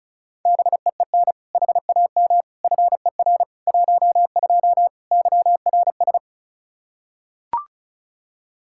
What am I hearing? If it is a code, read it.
Morse code, heard as a keyed tone, about 36 wpm, 700 Hz. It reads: BEEN HAM FER 12 YRS
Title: Practice QSO - Exchange 3 - More Info - 35wpm